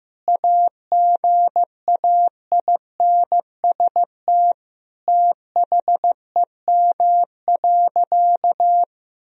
Morse code, heard as a keyed tone, about 16 wpm, 700 Hz. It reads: AGAINST THEM.
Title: Sentences from Top 1000 Words - Review Part 4 - 15wpm